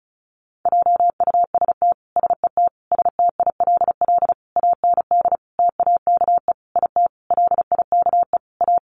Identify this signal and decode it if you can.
Morse code, heard as a keyed tone, about 34 wpm, 700 Hz. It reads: JUST SET STILL AND TAKE IT LIKE A
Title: Sentences from Top 300 Words 35wpm